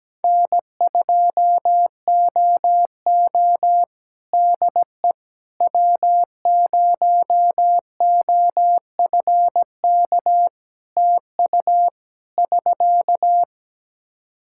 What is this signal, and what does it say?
Morse code, heard as a keyed tone, about 17 wpm, 700 Hz. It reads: N2OO DE W0OFK TU <SK>
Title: K1USN SST QSO Practice 17wpm